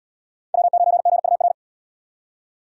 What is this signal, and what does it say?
Morse code, heard as a keyed tone, about 72 wpm, 700 Hz. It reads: W9GUR